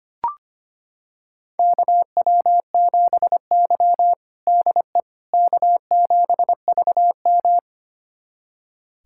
Morse code, heard as a keyed tone, about 25 wpm, 700 Hz. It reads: KW7Y DE K7VM
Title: USA Callsign DE Callsign - 4 Characters - 25wpm